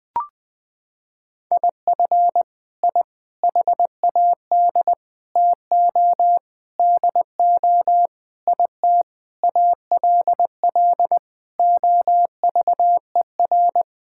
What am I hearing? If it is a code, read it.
Morse code, heard as a keyed tone, about 21 wpm, 700 Hz. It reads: IF I HAD TO DO IT ALL OVER
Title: Sets of 9 Words - 20wpm